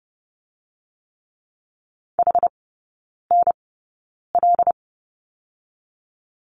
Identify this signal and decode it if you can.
Morse code, heard as a keyed tone, about 30 wpm, spaced as for 9 wpm, 690 Hz. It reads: HNL
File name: Lesson 17e - Letter u - Three Characters 30wpm